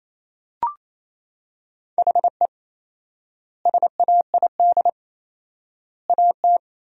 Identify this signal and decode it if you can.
Morse code, heard as a keyed tone, about 28 wpm, 700 Hz. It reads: HE SAID AT
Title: 4x Word-Spacing - Sentences from Top 300 Words - 28wpm